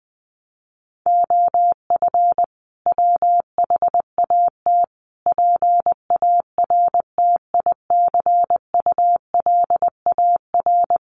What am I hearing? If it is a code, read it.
Morse code, heard as a keyed tone, about 20 wpm, 700 Hz. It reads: OF WHAT PARTICULAR
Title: Sentences from Top 1300 Words 20wpm